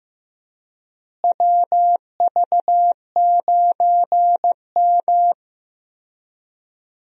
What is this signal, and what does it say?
Morse code, heard as a keyed tone, about 15 wpm, 700 Hz. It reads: WV9M
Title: USA Call Signs - 4 Characters - Repeated 2x - 15wpm